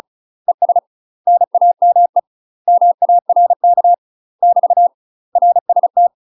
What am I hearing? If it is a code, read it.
Morse code, heard as a keyed tone, about 36 wpm, 700 Hz. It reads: ES NAME MARK <BT> RST
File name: UK MASTER QSOs TRAINING / 35WPM / UK MASTER QSOs 0007